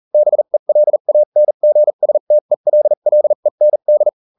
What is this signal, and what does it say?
Morse code, heard as a keyed tone, about 40 wpm, spaced as for 35 wpm, 600 Hz. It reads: BELANGSTELLEND